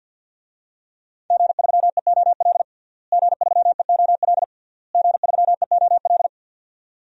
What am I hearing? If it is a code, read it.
Morse code, heard as a keyed tone, about 50 wpm, 700 Hz. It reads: G3EOL G3EOL G3EOL
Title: DXCC-100 Call Signs - 5 Characters - Repeated 3x - 50wpm